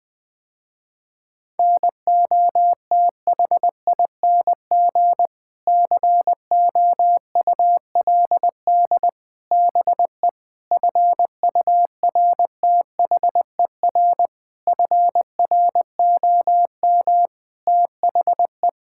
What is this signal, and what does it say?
Morse code, heard as a keyed tone, about 21 wpm, 700 Hz. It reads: NOTHING COULD BE FURTHER FROM THE
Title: Sentences with 6-Word Phrases - Spoken Beginning - 20wpm